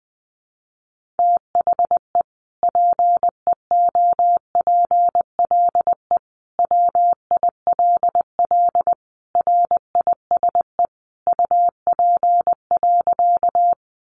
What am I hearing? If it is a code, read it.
Morse code, heard as a keyed tone, about 20 wpm, 700 Hz. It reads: THE PEOPLE WILL RISE UP.
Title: Sentences from Top 900 Words 20wpm